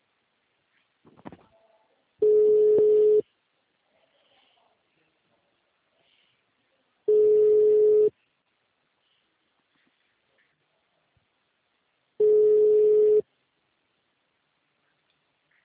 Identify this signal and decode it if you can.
Ringback tone (Europe, 425 Hz).
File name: out- -2017- - - .3749